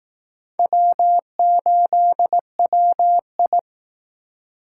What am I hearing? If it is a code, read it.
Morse code, heard as a keyed tone, about 18 wpm, 700 Hz. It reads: W8WI
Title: Speed Racing - USA 1x2 Call Signs 15wpm